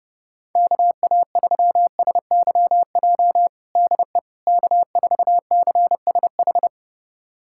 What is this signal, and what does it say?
Morse code, heard as a keyed tone, about 32 wpm, 700 Hz. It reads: KA3SYJ DE K4CSH
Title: USA Callsign DE Callsign 30wpm